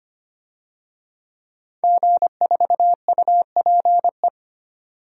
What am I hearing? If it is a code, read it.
Morse code, heard as a keyed tone, about 25 wpm, 700 Hz. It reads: G4UPE
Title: DXCC-100 Call Signs - 5 Characters - 25wpm